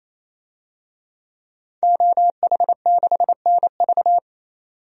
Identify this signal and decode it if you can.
Morse code, heard as a keyed tone, about 28 wpm, 700 Hz. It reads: OH6NV
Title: DXCC-100 Call Signs - 5 Characters - 28wpm